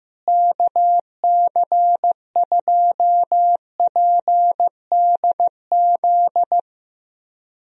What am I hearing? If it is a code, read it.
Morse code, heard as a keyed tone, about 14 wpm, 700 Hz. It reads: KC2PDZ
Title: USA Call Signs - 6 Characters - Repeated 3x - 15wpm